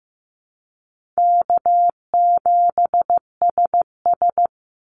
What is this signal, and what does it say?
Morse code, heard as a keyed tone, about 15 wpm, 700 Hz. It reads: K7SS
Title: Easy Call Signs - 15wpm